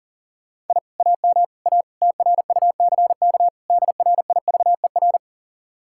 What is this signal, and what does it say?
Morse code, heard as a keyed tone, about 51 wpm, 700 Hz. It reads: I AM A TRUCK DRIVER